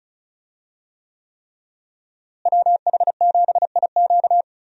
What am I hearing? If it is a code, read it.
Morse code, heard as a keyed tone, about 36 wpm, 700 Hz. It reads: WH7IQ
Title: USA Call Signs - 5 Characters - Repeated 3x - 35wpm